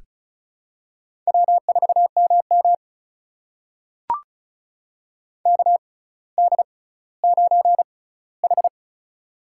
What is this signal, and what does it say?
Morse code, heard as a keyed tone, about 36 wpm, spaced as for 18 wpm, 700 Hz. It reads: W4MM KD9H
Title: USA Call Signs - 4 Characters - 6x Character Spacing - 35wpm